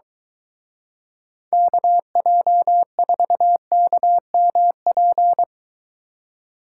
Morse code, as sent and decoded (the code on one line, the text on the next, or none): -.- .--- ....- -.- -- .--.
KJ4KMP